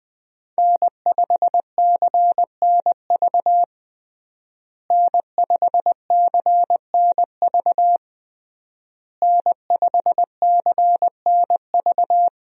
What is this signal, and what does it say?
Morse code, heard as a keyed tone, about 20 wpm, 700 Hz. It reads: N5CNV N5CNV N5CNV
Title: USA Call Signs - 5 Characters - Repeated 3x - 20wpm